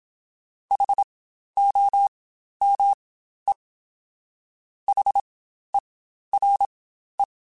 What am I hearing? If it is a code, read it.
Morse code, heard as a keyed tone, about 27 wpm, spaced as for 12 wpm, 800 Hz. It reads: HOME HERE
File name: Spaced Alphas / Spaced Alphas mp3s / Hs sp @25